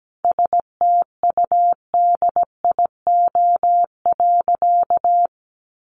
Morse code, heard as a keyed tone, about 17 wpm, 700 Hz. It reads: STUDIO.
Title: Sentences from Top 1200 Words 17wpm